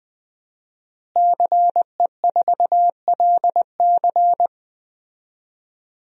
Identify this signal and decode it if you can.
Morse code, heard as a keyed tone, about 20 wpm, 700 Hz. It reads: CE4LC
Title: DXCC-100 Call Signs - 5 Characters - Repeated 2x - 20wpm